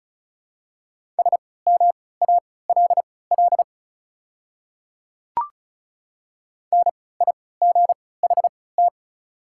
Morse code, heard as a keyed tone, about 36 wpm, spaced as for 20 wpm, 690 Hz. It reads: SMALL NIGHT
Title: 3x Character Spacing - Top 300 Words - 35wpm